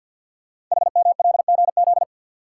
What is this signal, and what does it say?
Morse code, heard as a keyed tone, about 50 wpm, 700 Hz. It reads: HMLGZ
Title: Code Groups - 5 Letters - 50wpm